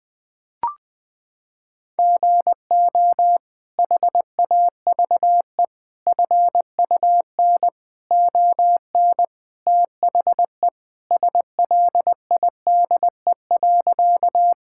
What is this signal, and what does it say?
Morse code, heard as a keyed tone, about 19 wpm, 700 Hz. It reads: GO HAVE FUN ON THE SLIDE.
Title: Sentences from Top 1600 Words 20wpm